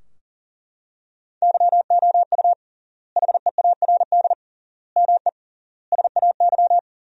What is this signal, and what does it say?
Morse code, heard as a keyed tone, about 40 wpm, 700 Hz. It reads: YOU HEARD ME SAY
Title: 3x Word-Spacing - Sentences from Top 500 Words - 40wpm